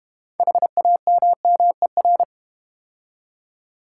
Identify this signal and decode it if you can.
Morse code, heard as a keyed tone, about 30 wpm, 700 Hz. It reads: HAMMER
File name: Easy Words - Speed-Racing with Hints - 25wpm